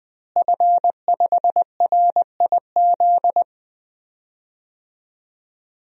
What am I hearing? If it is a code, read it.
Morse code, heard as a keyed tone, about 20 wpm, 700 Hz. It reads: F5RIZ
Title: DXCC-100 Call Signs - 5 Characters - Repeated 3x - 20wpm